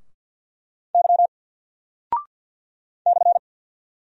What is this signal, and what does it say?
Morse code, heard as a keyed tone, about 48 wpm, 700 Hz. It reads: Y /
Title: Single Character (Letter:Number:Punctuation:Prosign) 50wpm